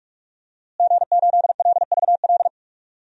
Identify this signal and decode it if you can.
Morse code, heard as a keyed tone, about 46 wpm, 700 Hz. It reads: G8LVL
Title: DXCC-100 Call Signs - 5 Characters - Repeated 3x - 45wpm